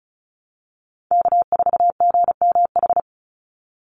Morse code, heard as a keyed tone, about 34 wpm, 700 Hz. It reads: K4GMH
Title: Call Signs Repeated 2x - 35wpm